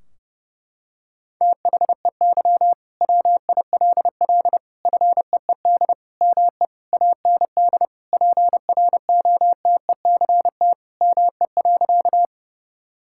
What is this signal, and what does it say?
Morse code, heard as a keyed tone, about 32 wpm, 700 Hz. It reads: THEY WILL FEED ME AND PROTECT ME.
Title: Sentences from Top 1600 Words 30wpm